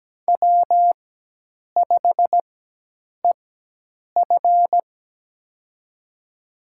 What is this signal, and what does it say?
Morse code, heard as a keyed tone, about 17 wpm, spaced as for 8 wpm, 700 Hz. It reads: W5EF
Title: USA Call Signs - 4 Characters - 4x Character Spacing - 17wpm